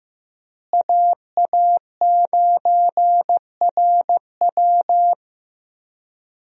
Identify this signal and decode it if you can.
Morse code, heard as a keyed tone, about 16 wpm, 700 Hz. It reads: AA9RW
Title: USA Call Signs - 5 Characters - Repeated 3x - 15wpm